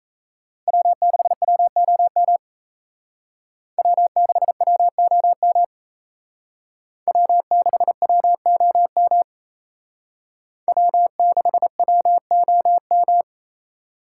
Morse code, heard as a keyed tone, about 34 wpm, 700 Hz. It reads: W6WOM W6WOM W6WOM W6WOM
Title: Speed Racing - USA Call Signs 28wpm